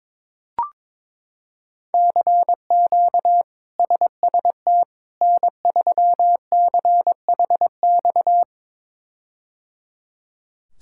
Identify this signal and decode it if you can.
Morse code, heard as a keyed tone, about 22 wpm, 700 Hz. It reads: CQ SST N3CHX
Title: K1USN SST QSO Practice - 22wpm